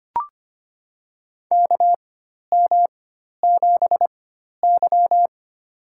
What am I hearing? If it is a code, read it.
Morse code, heard as a keyed tone, about 25 wpm, spaced as for 12 wpm, 700 Hz. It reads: KM7Y
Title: USA Call Signs - 4 Characters - 4x Character Spacing - 25wpm